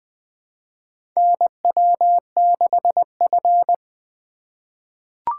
Morse code, heard as a keyed tone, about 20 wpm, 700 Hz. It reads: NW6F
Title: USA Call Signs - 4 Characters - 2x Character Spacing - 20wpm